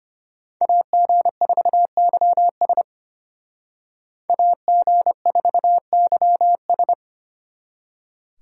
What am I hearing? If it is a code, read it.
Morse code, heard as a keyed tone, about 33 wpm, 700 Hz. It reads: AG4YS AG4YS